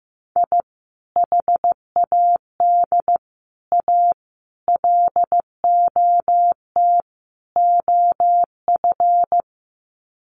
Morse code, heard as a keyed tone, about 14 wpm, 700 Hz. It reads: I HAD A LOT OF
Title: Sets of 5 Words Encore 15wpm